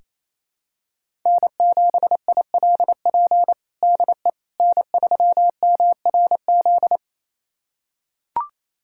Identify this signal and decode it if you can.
Morse code, heard as a keyed tone, about 28 wpm, 700 Hz. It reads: N7ILP DE N3MRZ
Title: USA Callsign DE Callsign - 5 Characters - 28wpm